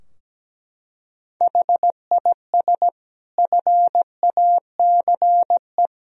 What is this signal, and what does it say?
Morse code, heard as a keyed tone, about 17 wpm, 700 Hz. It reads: HIS FACE
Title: Sentences from Top 1000 Words - Review Part 2 - 17wpm